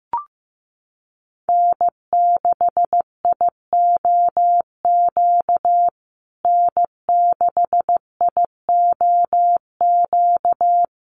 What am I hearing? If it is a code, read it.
Morse code, heard as a keyed tone, about 15 wpm, 700 Hz. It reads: N6IOQ N6IOQ
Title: USA Call Signs Repeated 3x 15wpm